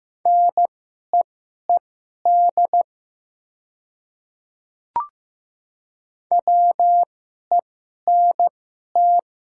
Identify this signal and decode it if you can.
Morse code, heard as a keyed tone, about 15 wpm, spaced as for 11 wpm, 700 Hz. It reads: NEED WENT
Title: 2x Character Spacing - Top 300 Words - 15wpm